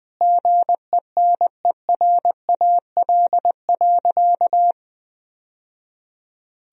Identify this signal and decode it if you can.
Morse code, heard as a keyed tone, about 20 wpm, 700 Hz. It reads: GENERAL.
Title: Sentences from Top 1000 Words - Review Part 3 - 20wpm